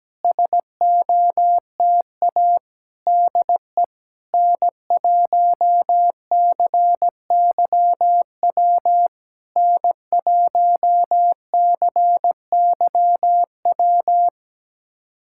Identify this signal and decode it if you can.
Morse code, heard as a keyed tone, about 17 wpm, 700 Hz. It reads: SOTA DE N1CYW N1CYW
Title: SOTA QSO Practice 17wpm